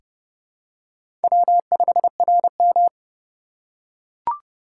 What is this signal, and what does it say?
Morse code, heard as a keyed tone, about 30 wpm, 690 Hz. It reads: W5RM
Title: USA Call Signs - 4 Characters - 2x Character Spacing - 30wpm